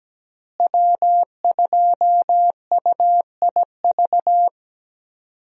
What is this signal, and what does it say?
Morse code, heard as a keyed tone, about 17 wpm, 700 Hz. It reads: W2UIV
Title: USA Call Signs - 5 Characters - Repeated 3x - 17wpm